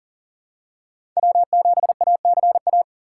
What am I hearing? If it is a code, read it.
Morse code, heard as a keyed tone, about 37 wpm, 700 Hz. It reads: W7ACA
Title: USA Call Signs - 5 Characters - 3x Character Spacing - 40wpm